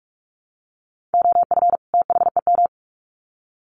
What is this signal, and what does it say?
Morse code, heard as a keyed tone, about 44 wpm, 700 Hz. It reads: OF THEM
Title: Sets of 2 Words 45wpm